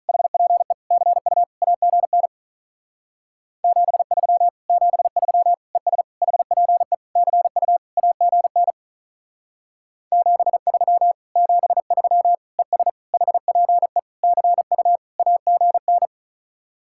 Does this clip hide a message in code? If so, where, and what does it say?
Yes: Morse code from 0.1 to 16.1 s, heard as a keyed tone, about 44 wpm, 700 Hz. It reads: HPE CU AGN 73 73 ES HPE CU AGN 73 73 ES HPE CU AGN